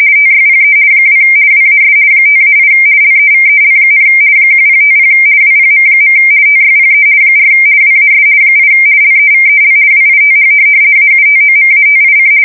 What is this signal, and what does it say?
RTTY, about 50 baud. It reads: TYPE 2 WAS 5-8 LEVEL, 1050 WPM. TYPE 4 WAS 8 LEVEL WITH AUTOMATIC
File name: fsk170-50B / teletype-corp